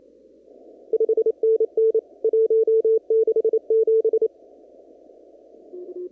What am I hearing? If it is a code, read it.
Morse code, heard as a keyed tone, about 28 wpm, 440 Hz. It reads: 5NN 167
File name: CW / kt5j wpx cw-op-k5pi